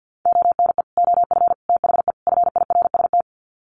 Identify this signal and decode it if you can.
Morse code, heard as a keyed tone, about 50 wpm, 700 Hz. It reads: ONE OF THE FIRST
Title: Sets of 4 Words 50wpm